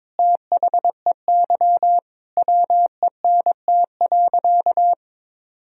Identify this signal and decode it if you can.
Morse code, heard as a keyed tone, about 22 wpm, 700 Hz. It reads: THEY WENT.
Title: Sentences from Top 1000 Words - Review Part 3 - 22wpm